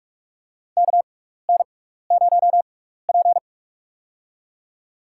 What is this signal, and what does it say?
Morse code, heard as a keyed tone, about 44 wpm, spaced as for 15 wpm, 700 Hz. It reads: KN0P
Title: USA Call Signs - 6x Character Spacing - 45wpm